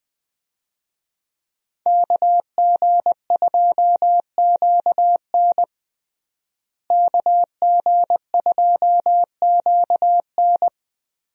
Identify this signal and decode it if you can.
Morse code, heard as a keyed tone, about 21 wpm, 700 Hz. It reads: KG2QN KG2QN
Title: USA Call Signs - 5 Characters - Repeated 2x - 20wpm